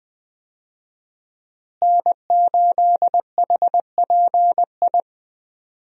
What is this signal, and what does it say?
Morse code, heard as a keyed tone, about 21 wpm, 700 Hz. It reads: N8HPI
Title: USA Call Signs - 5 Characters - Repeated 2x - 20wpm